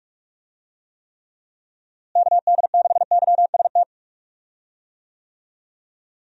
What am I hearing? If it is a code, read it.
Morse code, heard as a keyed tone, about 46 wpm, 700 Hz. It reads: KD6YST
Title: USA Call Signs - 6 Characters - Repeated 3x - 45wpm